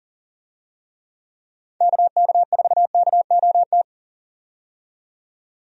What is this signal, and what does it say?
Morse code, heard as a keyed tone, about 40 wpm, 700 Hz. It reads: KK4KOT
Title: USA Call Signs - 6 Characters - 40wpm